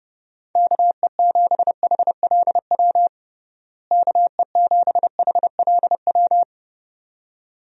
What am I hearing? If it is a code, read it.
Morse code, heard as a keyed tone, about 32 wpm, 700 Hz. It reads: KE7HLW KE7HLW
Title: USA Call Signs - 6 Characters - Repeated 2x - 30wpm